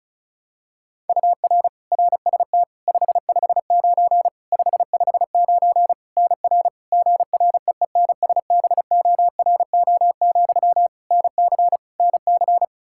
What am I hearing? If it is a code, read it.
Morse code, heard as a keyed tone, about 36 wpm, 700 Hz. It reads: UR RST 559 559 NR GREENSBORO, NC NC
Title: K1USN SST QSO Practice 35wpm